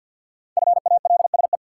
Morse code, heard as a keyed tone, about 46 wpm, 700 Hz. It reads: FALSE
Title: Sentences from Top 1600 Words - New Word Drill - 50wpm